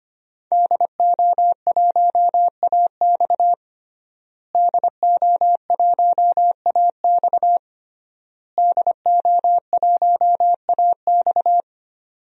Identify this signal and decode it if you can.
Morse code, heard as a keyed tone, about 25 wpm, 700 Hz. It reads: DO1AX DO1AX DO1AX
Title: DXCC-100 Call Signs - 5 Characters - Repeated 3x - 25wpm